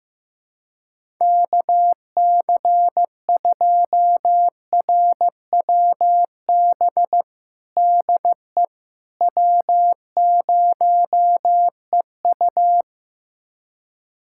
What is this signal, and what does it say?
Morse code, heard as a keyed tone, about 16 wpm, 700 Hz. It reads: KC2RWB DE W0EU
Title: USA Callsign DE Callsign 15wpm